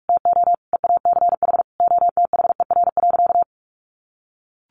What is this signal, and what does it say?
Morse code, heard as a keyed tone, about 44 wpm, 700 Hz. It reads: TO EACH OTHER.
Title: Additional Easy Phrases 45wpm